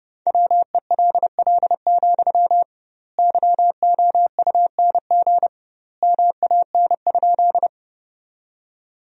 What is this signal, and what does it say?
Morse code, heard as a keyed tone, about 30 wpm, 700 Hz. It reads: WELL, YOUNG MAN?
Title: 2x Word-Spacing - Sentences from Top 400 Words - 30wpm